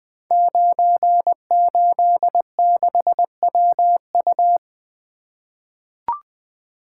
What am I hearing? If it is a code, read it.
Morse code, heard as a keyed tone, about 20 wpm, 700 Hz. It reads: 986WU
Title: Code Groups - 5 Letter:Numbers - 20wpm